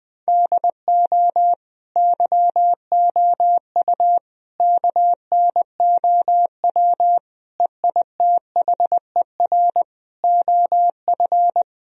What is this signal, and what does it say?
Morse code, heard as a keyed tone, about 20 wpm, 700 Hz. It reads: DO YOU KNOW EITHER OF
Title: Sentences from Top 1000 Words - Review Part 4 - 20wpm